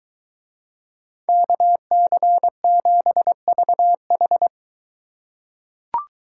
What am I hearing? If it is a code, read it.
Morse code, heard as a keyed tone, about 21 wpm, 700 Hz. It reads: KC7VH